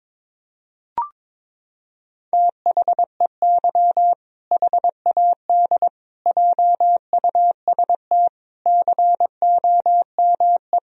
Morse code, heard as a keyed tone, about 22 wpm, 700 Hz. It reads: THEY HAD JUST COME
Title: Sentences from Top 1000 Words - Review Part 2 - 22wpm